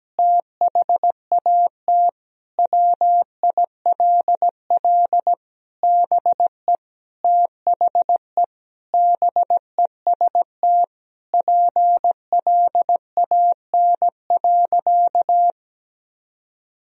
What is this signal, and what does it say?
Morse code, heard as a keyed tone, about 17 wpm, 700 Hz. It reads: THAT WILL BE THE BEST PLAN.
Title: Sentences from Top 1000 Words - Review Part 2 - 17wpm